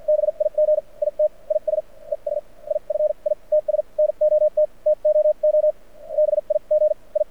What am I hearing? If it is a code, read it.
Morse code, heard as a keyed tone, about 48 wpm, spaced as for 40 wpm, 610 Hz. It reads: BIG IT IS ES IF ITS NOT TOO BIG I
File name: archived / CRSnet